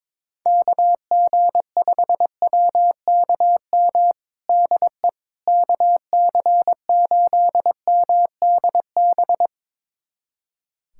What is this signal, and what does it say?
Morse code, heard as a keyed tone, about 21 wpm, 700 Hz. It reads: KG5WKM DE KC8MDB
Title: USA Callsign DE Callsign 22wpm